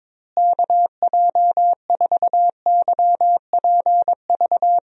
Morse code, heard as a keyed tone, about 22 wpm, 700 Hz. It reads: KJ4YPV
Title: USA Call Signs - 6 Characters - Repeated 3x - 22wpm